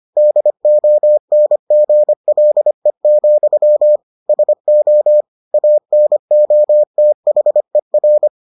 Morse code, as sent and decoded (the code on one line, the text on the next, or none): -.. --- -. --. .-.. --..-- / ... --- / .- -. --- - .... .-.
DONGLE, SO ANOTHER